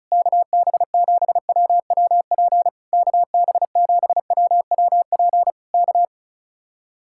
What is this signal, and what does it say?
Morse code, heard as a keyed tone, about 32 wpm, 700 Hz. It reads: KB7WWP KB7WWP K
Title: Speed-Racing Common QSO Phrases - 35wpm